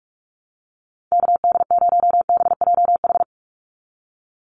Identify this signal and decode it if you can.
Morse code, heard as a keyed tone, about 44 wpm, 700 Hz. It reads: KD0BJH
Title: USA Call Signs 45wpm